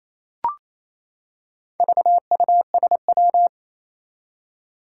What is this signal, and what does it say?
Morse code, heard as a keyed tone, about 28 wpm, 700 Hz. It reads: VUSW
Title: Four Challenging Letter-Number Combinations 28wpm